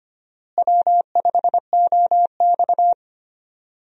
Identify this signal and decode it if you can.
Morse code, heard as a keyed tone, about 25 wpm, 700 Hz. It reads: W5OX
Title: USA Call Signs - 4 Characters - Repeated 2x - 25wpm